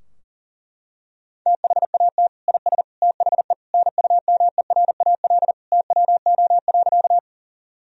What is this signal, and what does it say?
Morse code, heard as a keyed tone, about 40 wpm, 700 Hz. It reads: THAT IS THE NUMERAL TWO.
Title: Sentences from Top 1000 Words - Review Part 2 - 40wpm